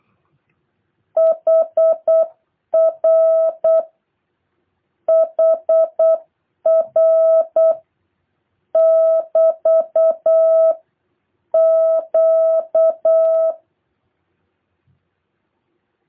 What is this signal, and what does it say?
Morse code, heard as a keyed tone, about 8 wpm, 650 Hz. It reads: HR HR <BT> Q